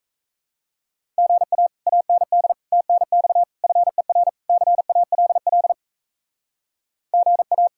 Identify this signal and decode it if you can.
Morse code, heard as a keyed tone, about 34 wpm, 700 Hz. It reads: GA AND TNX FER CALL GA
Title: Speed-Racing Common QSO Phrases - 28wpm